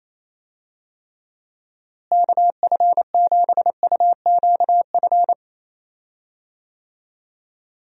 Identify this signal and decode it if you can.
Morse code, heard as a keyed tone, about 28 wpm, 700 Hz. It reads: KF7UQF
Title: USA Call Signs - 6 Characters - Repeated 2x - 28wpm